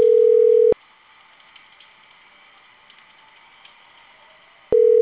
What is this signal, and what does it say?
Ringback tone (North America), 440 + 480 Hz.
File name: CENTRAL CAEA / out agcaea02 - .12288